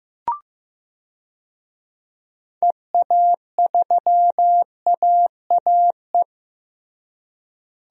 Morse code, heard as a keyed tone, about 15 wpm, 700 Hz. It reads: EA3AAE
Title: DXCC-100 Call Signs - 6 Characters - Repeated 3x - 15wpm